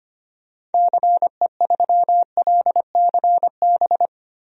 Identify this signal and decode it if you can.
Morse code, heard as a keyed tone, about 25 wpm, 700 Hz. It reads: CE3LCB
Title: DXCC-100 Call Signs - 6 Characters - 25wpm